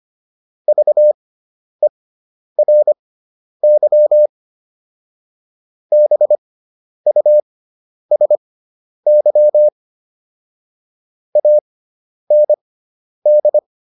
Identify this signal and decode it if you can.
Morse code, heard as a keyed tone, about 25 wpm, spaced as for 10 wpm, 600 Hz. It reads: VERY BUSY AND